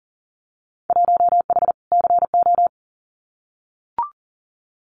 Morse code, heard as a keyed tone, about 38 wpm, 700 Hz. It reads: 1H CO
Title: Winter Field Day Exchanges 40wpm